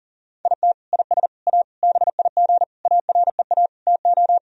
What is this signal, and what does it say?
Morse code, heard as a keyed tone, about 40 wpm, 700 Hz. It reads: IT IS A BIG AREA TO
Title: Sentences from Top 1000 Words - Review Part 2 - 40wpm